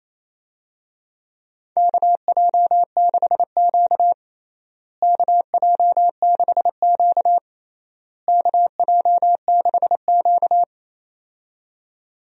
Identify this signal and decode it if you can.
Morse code, heard as a keyed tone, about 28 wpm, 700 Hz. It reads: KJ6Q KJ6Q KJ6Q
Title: USA Call Signs - 4 Characters - Repeated 3x - 28wpm